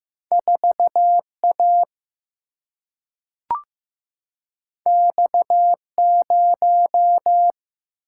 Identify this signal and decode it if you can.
Morse code, heard as a keyed tone, about 15 wpm, 700 Hz. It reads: 4 A X 0
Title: Single-Letter Single-Number 15wpm